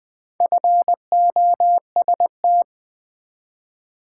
Morse code, heard as a keyed tone, about 20 wpm, 700 Hz. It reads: FOST
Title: Build A Word - 2 Letters At A Time - 6-Letter Words - 20wpm